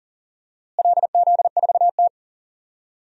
Morse code, heard as a keyed tone, about 40 wpm, 700 Hz. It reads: LZ4T
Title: DXCC-100 Call Signs - 4 Characters - Repeated 3x - 40wpm